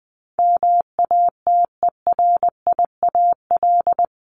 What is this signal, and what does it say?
Morse code, heard as a keyed tone, about 20 wpm, 700 Hz. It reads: MATERIAL
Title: ES:S Suffix - Top 100 Words with Base Words 20wpm